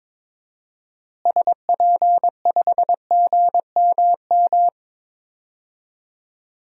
Morse code, heard as a keyed tone, about 22 wpm, 700 Hz. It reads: SP5GMM
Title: DXCC-100 Call Signs - 6 Characters - Repeated 2x - 22wpm